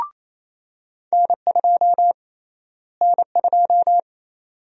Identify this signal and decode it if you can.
Morse code, heard as a keyed tone, about 27 wpm, 700 Hz. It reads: N2 N2
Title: USA Call Sign Prefixes - Repeated 2x - 28wpm